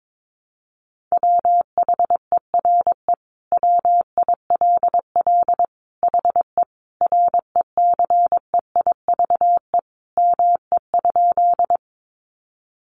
Morse code, heard as a keyed tone, about 22 wpm, 700 Hz. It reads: WHERE WILL HE RECEIVE ME?
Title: Sentences from Top 1000 Words 22wpm